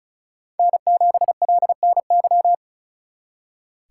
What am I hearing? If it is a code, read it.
Morse code, heard as a keyed tone, about 43 wpm, 700 Hz. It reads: N7LNY
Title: Speed Racing - USA 1x3 Call Signs 35wpm